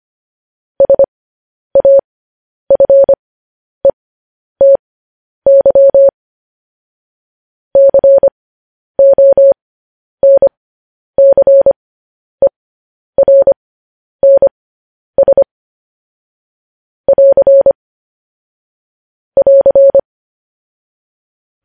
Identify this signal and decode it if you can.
Morse code, heard as a keyed tone, about 25 wpm, spaced as for 10 wpm, 550 Hz. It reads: SAFETY CONCERNS <AR> <AR>
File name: podcast / News headlines-1x-25x10